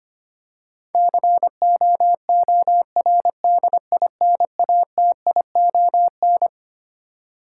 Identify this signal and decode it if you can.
Morse code, heard as a keyed tone, about 25 wpm, 700 Hz. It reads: COORDINATION
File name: Twelve Letter Words - 25wpm